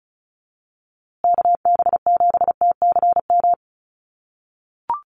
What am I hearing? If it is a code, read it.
Morse code, heard as a keyed tone, about 34 wpm, 700 Hz. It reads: KB7TCM
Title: USA Call Signs Repeated 2x 35wpm